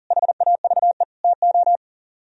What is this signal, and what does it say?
Morse code, heard as a keyed tone, about 52 wpm, 700 Hz. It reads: HAVE TO